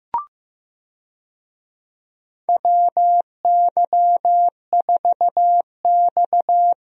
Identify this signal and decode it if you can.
Morse code, heard as a keyed tone, about 15 wpm, 700 Hz. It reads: WY4X
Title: USA Call Signs - 4 Characters - Repeated 2x - 15wpm